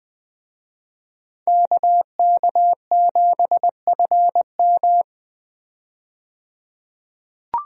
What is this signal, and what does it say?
Morse code, heard as a keyed tone, about 20 wpm, 700 Hz. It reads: KK7FM
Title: USA Call Signs - 5 Characters - 20wpm